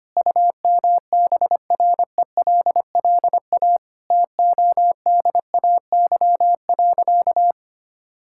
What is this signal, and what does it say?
Morse code, heard as a keyed tone, about 25 wpm, 700 Hz. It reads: UMBRELLA TODAY.
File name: Easy Sentences with Hints - 25wpm